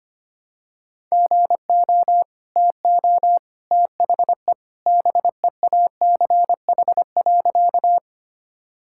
Morse code, heard as a keyed tone, about 25 wpm, 690 Hz. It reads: GO TO THE BEACH.
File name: Easy Sentences with Hints - 25wpm